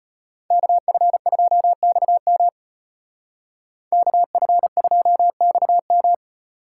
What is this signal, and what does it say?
Morse code, heard as a keyed tone, about 33 wpm, 700 Hz. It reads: KF2XM KF2XM